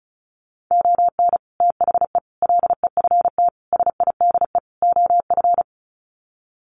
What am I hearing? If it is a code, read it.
Morse code, heard as a keyed tone, about 34 wpm, 700 Hz. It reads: ON THE LEFT SIDE OF
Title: Sets of 5 Words Encore2 35wpm